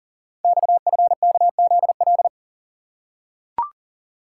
Morse code, heard as a keyed tone, about 41 wpm, 690 Hz. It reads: XFKZL
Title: Code Groups - 5 Letters - 40wpm